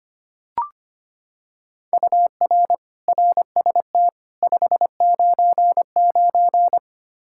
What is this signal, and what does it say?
Morse code, heard as a keyed tone, about 25 wpm, 700 Hz. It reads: UR RST 599
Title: K1USN SST QSO Practice 25wpm